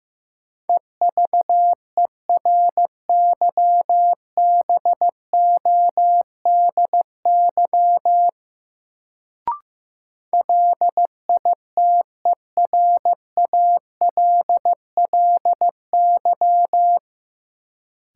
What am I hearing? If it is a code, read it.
Morse code, heard as a keyed tone, about 15 wpm, 700 Hz. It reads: EVERYBODY LITERALLY
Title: Sentences from Top 1400 Words - New Word Drill Alt - 15wpm